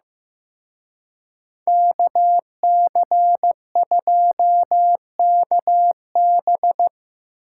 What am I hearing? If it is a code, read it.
Morse code, heard as a keyed tone, about 18 wpm, 700 Hz. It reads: KC2KB
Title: Speed Racing - USA Call Signs 15wpm